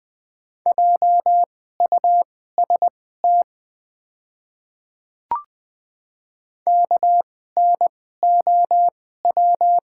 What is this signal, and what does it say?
Morse code, heard as a keyed tone, about 20 wpm, spaced as for 14 wpm, 700 Hz. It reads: JUST KNOW
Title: 2x Character Spacing - Top 200 Words - 20wpm